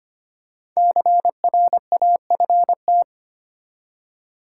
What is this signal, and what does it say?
Morse code, heard as a keyed tone, about 25 wpm, 700 Hz. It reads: CRAFT
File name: 5 Letter Words - First 3 Letters Given - 25wpm